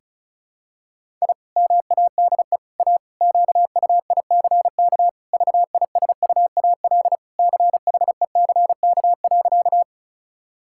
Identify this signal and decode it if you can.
Morse code, heard as a keyed tone, about 36 wpm, 700 Hz. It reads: I MADE A QUICK VISUAL CHECK.
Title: Sentences from Top 1000 Words - Review Part 6 - 35wpm